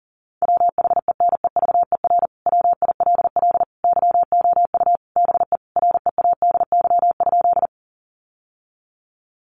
Morse code, heard as a keyed tone, about 40 wpm, 700 Hz. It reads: WHENEVER WILL YOU BE READY?
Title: Sentences from Top 1200 Words 40wpm